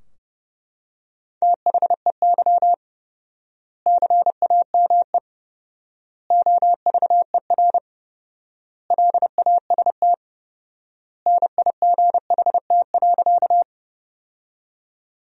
Morse code, heard as a keyed tone, about 32 wpm, 700 Hz. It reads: THEY CAME OVER LAST NIGHT.
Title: 4x Word-Spacing - Sentences from Top 300 Words - 30wpm